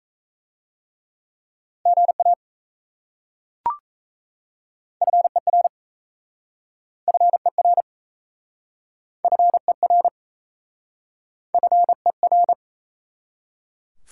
Morse code, heard as a keyed tone, about 36 wpm, 700 Hz. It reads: GA FER FER FER FER